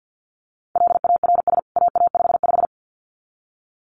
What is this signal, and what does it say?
Morse code, heard as a keyed tone, about 50 wpm, 700 Hz. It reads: LARS AA55
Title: CWT Exchange 50wpm